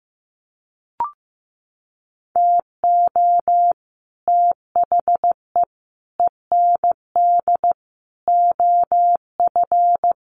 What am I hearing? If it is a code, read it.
Morse code, heard as a keyed tone, about 15 wpm, 700 Hz. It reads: TO THE END OF
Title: Sets of 4 Words 15wpm